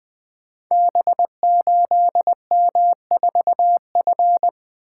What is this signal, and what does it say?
Morse code, heard as a keyed tone, about 20 wpm, 700 Hz. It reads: B8M4F
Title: Code Groups - 5 Letter:Numbers - 20wpm